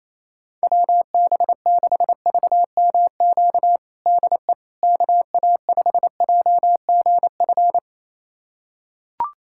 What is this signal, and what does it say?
Morse code, heard as a keyed tone, about 28 wpm, 700 Hz. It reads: WB6VMQ DE KA5JGF
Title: USA Callsign DE Callsign - 6 Characters - 28wpm